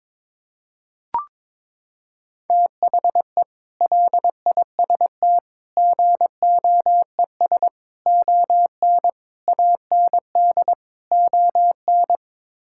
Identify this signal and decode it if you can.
Morse code, heard as a keyed tone, about 22 wpm, 700 Hz. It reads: THE LIST GOES ON AND ON
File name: practice / 22wpm / Sets of 6 Words - 22wpm